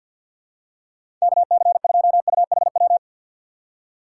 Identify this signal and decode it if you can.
Morse code, heard as a keyed tone, about 47 wpm, 700 Hz. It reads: KC2UHW
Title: USA Call Signs - 6 Characters - 3x Character Spacing - 50wpm